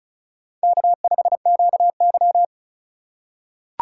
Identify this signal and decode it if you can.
Morse code, heard as a keyed tone, about 34 wpm, 690 Hz. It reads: K5QY
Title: USA Call Signs - 4 Characters - 2x Character Spacing - 35wpm